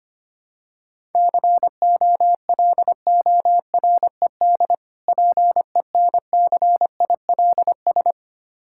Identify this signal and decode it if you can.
Morse code, heard as a keyed tone, about 25 wpm, 700 Hz. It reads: COLORED PENCILS
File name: practice / 25wpm / Difficult Words with Hints - 25wpm